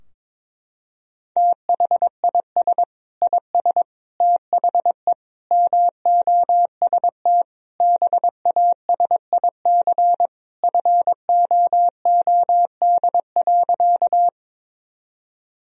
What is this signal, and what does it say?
Morse code, heard as a keyed tone, about 22 wpm, 700 Hz. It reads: THIS IS THE MOST BASIC FOOD.
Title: Sentences from Top 1000 Words - Review Part 3 - 22wpm